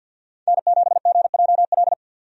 Morse code, heard as a keyed tone, about 49 wpm, 700 Hz. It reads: N7GJL
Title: USA Call Signs - 5 Characters - 50wpm